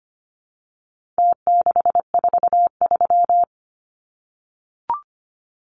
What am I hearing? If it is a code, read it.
Morse code, heard as a keyed tone, about 25 wpm, 700 Hz. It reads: T643
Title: Numbers - Contest Style - 4-Digit 25wpm